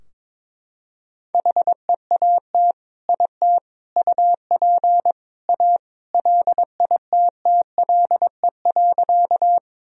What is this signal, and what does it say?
Morse code, heard as a keyed tone, about 22 wpm, 700 Hz. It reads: HEAT IT UP A LITTLE.
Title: Sentences from Top 1000 Words - Review Part 3 - 22wpm